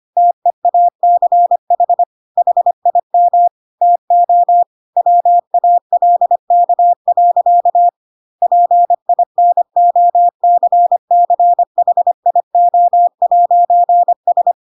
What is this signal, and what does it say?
Morse code, heard as a keyed tone, about 25 wpm, 700 Hz. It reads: TEACH HIM TO WALK. PINOCCHIO'S
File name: Pinocchio / Pinocchio ch3 25 25 ik1hge